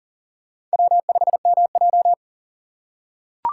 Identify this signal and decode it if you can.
Morse code, heard as a keyed tone, about 39 wpm, 700 Hz. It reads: W5MJ
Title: USA Call Signs - 4 Characters - 4x Character Spacing - 40wpm